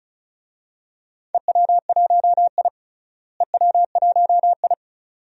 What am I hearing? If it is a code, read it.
Morse code, heard as a keyed tone, about 34 wpm, 700 Hz. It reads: EW1I EW1I
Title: DXCC-100 Call Signs - 4 Characters - Repeated 2x - 35wpm